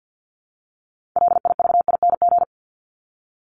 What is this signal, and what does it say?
Morse code, heard as a keyed tone, about 50 wpm, 700 Hz. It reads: LIVING
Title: ING Suffix - Top 100 Words with Base Words 50wpm